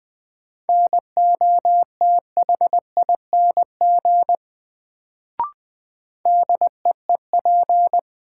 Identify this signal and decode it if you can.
Morse code, heard as a keyed tone, about 20 wpm, 700 Hz. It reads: NOTHING DEEP
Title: Sentences from Top 500 Words - New Word Drill - 20wpm